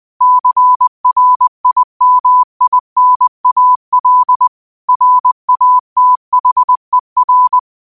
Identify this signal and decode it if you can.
Morse code, heard as a keyed tone, about 20 wpm, 1000 Hz. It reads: CRIMINAL RATHER
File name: qotdfiles / 20wpm